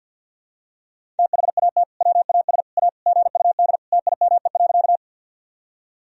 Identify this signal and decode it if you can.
Morse code, heard as a keyed tone, about 49 wpm, 700 Hz. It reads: THAT WAS A GUD TIME.
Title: Sentences from Top 200 Words and Abbreviations 50wpm